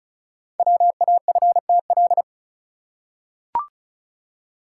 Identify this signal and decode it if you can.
Morse code, heard as a keyed tone, about 35 wpm, 690 Hz. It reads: WAFTL
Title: Code Groups - 5 Letters - 35wpm